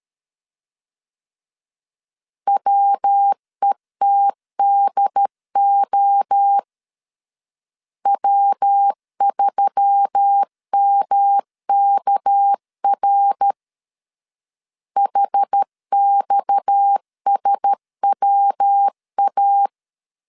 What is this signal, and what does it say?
Morse code, heard as a keyed tone, about 13 wpm, 800 Hz. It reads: WETDO W3MKR HXSWA
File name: CW / DET / 1 6 1500C S B 12wpm 3431